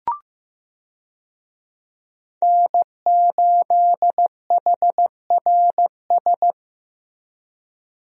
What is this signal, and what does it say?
Morse code, heard as a keyed tone, about 15 wpm, 700 Hz. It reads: N8HRS
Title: USA Call Signs - 5 Characters - Repeated 3x - 15wpm